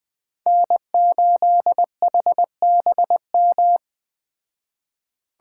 Morse code, heard as a keyed tone, about 19 wpm, 700 Hz. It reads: N8HBM
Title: Speed Racing - USA 1x3 Call Signs 17wpm